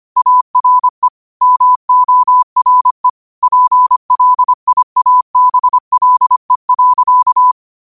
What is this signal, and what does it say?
Morse code, heard as a keyed tone, about 25 wpm, 1000 Hz. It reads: ARE MORE PLIABLE.
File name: qotdfiles / 25wpm